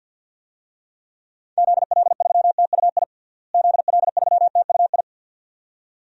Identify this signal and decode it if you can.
Morse code, heard as a keyed tone, about 52 wpm, 700 Hz. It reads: ZL3TUI ZL3TUI
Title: DXCC-100 Call Signs - 6 Characters - Repeated 2x - 50wpm